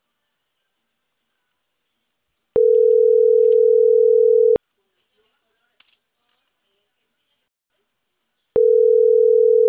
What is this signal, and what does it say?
Ringback tone (North America), 440 + 480 Hz.